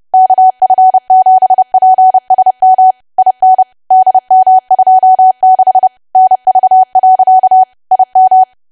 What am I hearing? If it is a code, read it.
Morse code, heard as a keyed tone, about 30 wpm, 750 Hz. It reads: KF7PSM IN DM26 NV. IM
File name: cwa / W1AW-Level3 / 30WPM